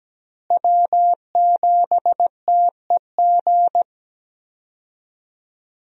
Morse code, heard as a keyed tone, about 17 wpm, 700 Hz. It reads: W7TEG
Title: USA Call Signs - 5 Characters - Repeated 3x - 17wpm